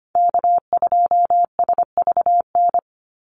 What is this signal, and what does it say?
Morse code, heard as a keyed tone, about 25 wpm, 700 Hz. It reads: K2SVN
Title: USA Call Signs 25wpm